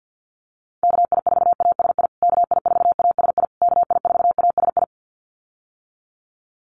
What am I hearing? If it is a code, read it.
Morse code, heard as a keyed tone, about 50 wpm, 700 Hz. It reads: KI4ASI KI4ASI KI4ASI
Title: USA Call Signs Repeated 3x 50wpm